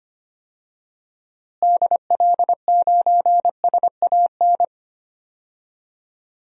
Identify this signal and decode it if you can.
Morse code, heard as a keyed tone, about 25 wpm, 700 Hz. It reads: DL9SAN
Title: DXCC-100 Call Signs - 6 Characters - Repeated 2x - 25wpm